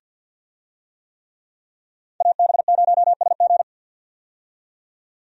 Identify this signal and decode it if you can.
Morse code, heard as a keyed tone, about 50 wpm, 700 Hz. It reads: AB0SG
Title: USA Call Signs - 5 Characters - 50wpm